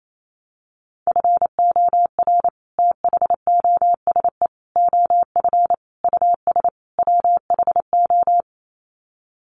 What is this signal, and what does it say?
Morse code, heard as a keyed tone, about 28 wpm, 700 Hz. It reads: FOR THOSE OF US WHO
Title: Sets of 5 Words Encore 28wpm